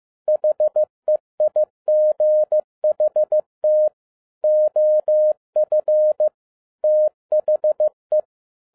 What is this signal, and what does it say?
Morse code, heard as a keyed tone, about 15 wpm, 610 Hz. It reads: HEIGHT OF THE
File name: 2012-01-mp3 / n07-ARA JR0DIL-15wpm